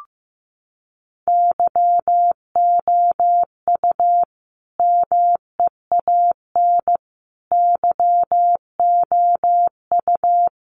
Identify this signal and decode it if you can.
Morse code, heard as a keyed tone, about 15 wpm, 700 Hz. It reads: YOU MEAN YOU
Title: Sentences from Top 1400 Words 15wpm